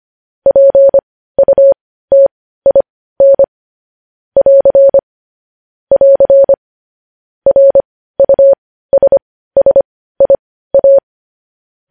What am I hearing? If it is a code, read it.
Morse code, heard as a keyed tone, about 25 wpm, spaced as for 15 wpm, 550 Hz. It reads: PUTIN <AR> <AR> RUSSIA
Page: News Headlines in Morse Code at 15 WPM